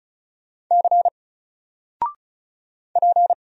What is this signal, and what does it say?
Morse code, heard as a keyed tone, about 35 wpm, 700 Hz. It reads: C P
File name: Single-Letter-Number-35wpm